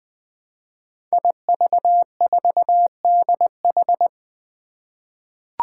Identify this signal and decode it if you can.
Morse code, heard as a keyed tone, about 20 wpm, 700 Hz. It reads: IV4DH
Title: Five Challenging Letter-Number Combinations 20wpm